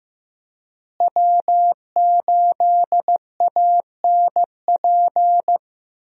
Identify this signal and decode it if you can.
Morse code, heard as a keyed tone, about 16 wpm, 700 Hz. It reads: W8ANP
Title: USA Call Signs - 5 Characters - Repeated 2x - 15wpm